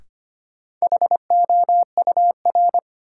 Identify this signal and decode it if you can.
Morse code, heard as a keyed tone, about 25 wpm, 700 Hz. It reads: HOUR
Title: Sentences from Top 400 Words - New Word Drill - 25wpm